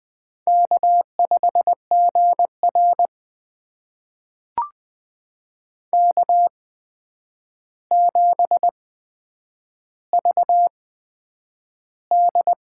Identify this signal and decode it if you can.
Morse code, heard as a keyed tone, about 20 wpm, spaced as for 9 wpm, 700 Hz. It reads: K5GR K7VD
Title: USA Call Signs - 4 Characters - 8x Character Spacing - 20wpm